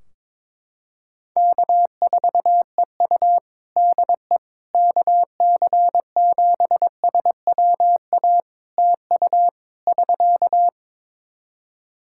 Morse code, heard as a keyed tone, about 22 wpm, 700 Hz. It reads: K4EU DE KC7SWA TU <SK>
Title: Practice QSO - Exchange 4 - Ending - 22wpm